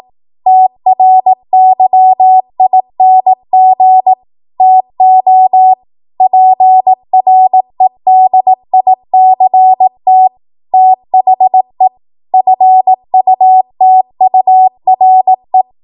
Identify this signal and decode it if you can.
Morse code, heard as a keyed tone, about 18 wpm, 750 Hz. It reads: TRYING TO PREDICT THE FUTURE
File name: Morse / 18WPM